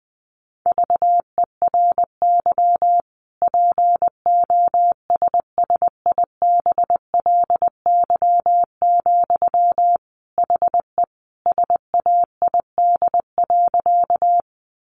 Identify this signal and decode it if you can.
Morse code, heard as a keyed tone, about 20 wpm, 700 Hz. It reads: VERY POSSIBLY, HE SAID.
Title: Sentences from Top 1300 Words 20wpm